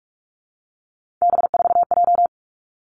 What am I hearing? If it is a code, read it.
Morse code, heard as a keyed tone, about 46 wpm, 700 Hz. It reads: B4J
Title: Three Challenging Letter-Number Combinations 45wpm